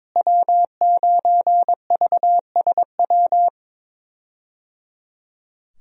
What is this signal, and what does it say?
Morse code, heard as a keyed tone, about 21 wpm, 700 Hz. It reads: W9VSW
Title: USA Call Signs - 5 Characters - Repeated 3x - 22wpm